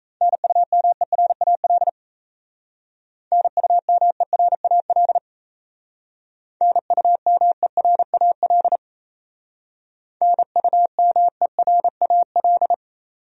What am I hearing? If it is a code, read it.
Morse code, heard as a keyed tone, about 36 wpm, 700 Hz. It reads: NUMERAL NUMERAL NUMERAL NUMERAL